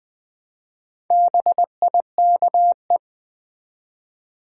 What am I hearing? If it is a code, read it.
Morse code, heard as a keyed tone, about 20 wpm, 700 Hz. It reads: BIKE
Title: Build A Word - 1 Letter At A Time - 4-Letter Words - 20wpm